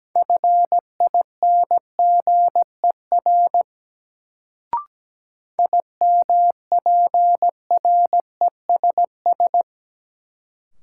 Morse code, heard as a keyed tone, about 17 wpm, 700 Hz. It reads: FINGER IMPRESS
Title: Sentences from Top 1600 Words - New Word Drill - 17wpm